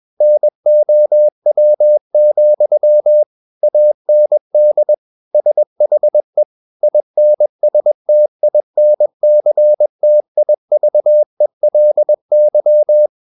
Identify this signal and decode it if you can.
Morse code, heard as a keyed tone, about 21 wpm, 600 Hz. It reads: NOW, AND SHE INSTINCTIVELY
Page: CW med Gnister